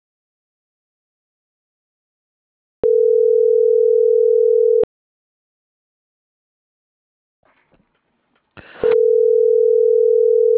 Ringback tone (North America), 440 + 480 Hz.